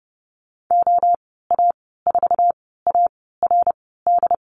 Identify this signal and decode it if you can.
Morse code, heard as a keyed tone, about 30 wpm, 700 Hz. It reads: O A 4 A R D
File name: Lesson 19g - Letter w - Call Signs 30wpm